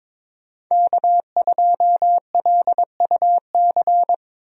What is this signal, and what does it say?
Morse code, heard as a keyed tone, about 22 wpm, 700 Hz. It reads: K2LUC
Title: USA Call Signs - 5 Characters - Repeated 3x - 22wpm